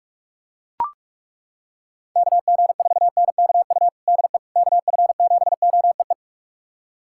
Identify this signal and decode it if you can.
Morse code, heard as a keyed tone, about 42 wpm, 700 Hz. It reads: KG4NKU DE KF7OEE
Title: USA Callsign DE Callsign - 6 Characters - 45wpm